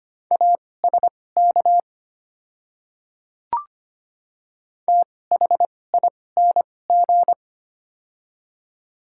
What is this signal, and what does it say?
Morse code, heard as a keyed tone, about 25 wpm, spaced as for 18 wpm, 700 Hz. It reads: ASK THING
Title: 2x Character Spacing - Top 200 Words - 25wpm